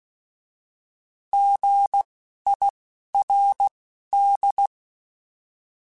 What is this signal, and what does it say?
Morse code, heard as a keyed tone, about 16 wpm, spaced as for 11 wpm, 800 Hz. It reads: GIRD
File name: Alphabetical / Alphabetical mp3s / Gs @10